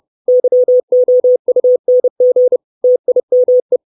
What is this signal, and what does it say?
Morse code, heard as a keyed tone, about 30 wpm, 500 Hz. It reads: YOUNG TIME
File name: 632 Words 30WPM